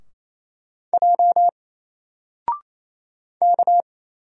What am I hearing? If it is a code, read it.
Morse code, heard as a keyed tone, about 27 wpm, 700 Hz. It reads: J K
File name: Single-Letter-Number-28wpm